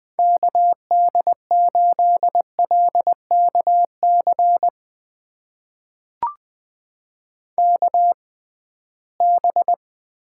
Morse code, heard as a keyed tone, about 20 wpm, spaced as for 15 wpm, 700 Hz. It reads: KD8LKC KB
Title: USA Call Signs - 6 Characters - 6x Character Spacing - 20wpm